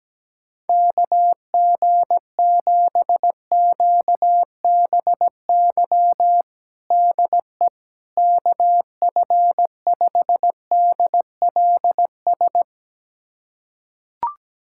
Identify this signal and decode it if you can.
Morse code, heard as a keyed tone, about 17 wpm, 700 Hz. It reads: KG7QBY DE KF5DLS
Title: USA Callsign DE Callsign - 6 Characters - 17wpm